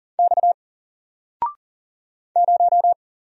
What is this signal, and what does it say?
Morse code, heard as a keyed tone, about 39 wpm, 700 Hz. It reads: X 0
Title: Single Character (Letter:Number:Punctuation:Prosign) 40wpm